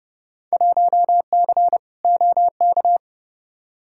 Morse code, heard as a keyed tone, about 30 wpm, 700 Hz. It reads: 1C OK
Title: Field Day Exchanges Encore - 30wpm